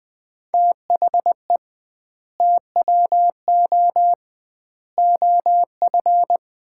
Morse code, heard as a keyed tone, about 21 wpm, 700 Hz. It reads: THE TWO OF
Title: 2x Word-Spacing - Sets of 4 Words - 20wpm